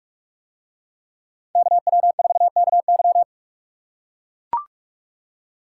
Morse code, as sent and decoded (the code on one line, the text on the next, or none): -.- .-- ....- -.- -.--
KW4KY